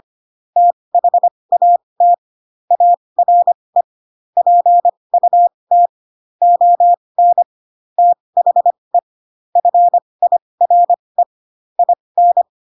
Morse code, heard as a keyed tone, about 25 wpm, spaced as for 20 wpm, 700 Hz. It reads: THAT ARE PUT ON THE FIRE IN
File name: Pinocchio / Pinocchio ch1 15 25 ik1hge